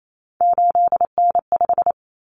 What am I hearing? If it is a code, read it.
Morse code, heard as a keyed tone, about 28 wpm, 700 Hz. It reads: 8N5
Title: Common Cut Numbers - 3-Digit 28wpm